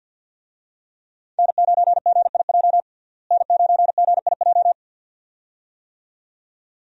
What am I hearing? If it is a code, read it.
Morse code, heard as a keyed tone, about 48 wpm, 700 Hz. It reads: N9GIJ N9GIJ
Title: USA Call Signs - 5 Characters - Repeated 2x - 50wpm